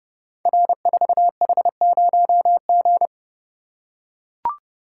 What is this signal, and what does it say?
Morse code, heard as a keyed tone, about 28 wpm, 700 Hz. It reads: R4H0G
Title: Code Groups - 5 Letter:Numbers - 30wpm